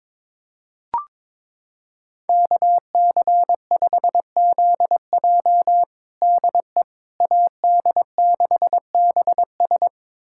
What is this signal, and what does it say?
Morse code, heard as a keyed tone, about 22 wpm, 700 Hz. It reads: KC5ZJ DE AD6BS
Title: USA Callsign DE Callsign - 5 Characters - 22wpm